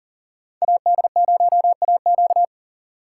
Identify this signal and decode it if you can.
Morse code, heard as a keyed tone, about 52 wpm, 700 Hz. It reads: AD0AQ